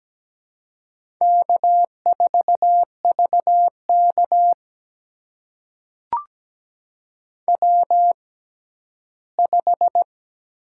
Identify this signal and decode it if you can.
Morse code, heard as a keyed tone, about 17 wpm, spaced as for 11 wpm, 700 Hz. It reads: K4VK W5
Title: USA Call Signs - 4 Characters - 6x Character Spacing - 17wpm